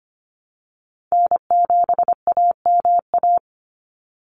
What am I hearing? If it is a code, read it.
Morse code, heard as a keyed tone, about 25 wpm, 700 Hz. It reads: N7AMA
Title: NA SOTA Chaser Callsigns 25wpm